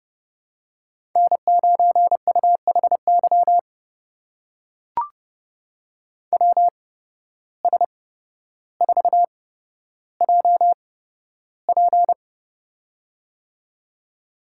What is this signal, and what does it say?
Morse code, heard as a keyed tone, about 30 wpm, spaced as for 13 wpm, 700 Hz. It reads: N9UHY WS4JP
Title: USA Call Signs - 5 Characters - 8x Character Spacing - 30wpm